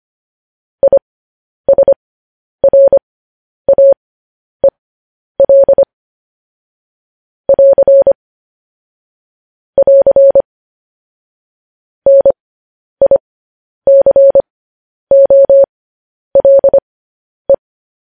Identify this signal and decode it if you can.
Morse code, heard as a keyed tone, about 25 wpm, spaced as for 10 wpm, 550 Hz. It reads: ISRAEL <AR> <AR> NICOLE
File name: podcast / News headlines-1x-25x10